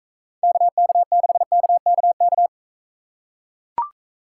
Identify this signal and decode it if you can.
Morse code, heard as a keyed tone, about 42 wpm, 700 Hz. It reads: KK6KKK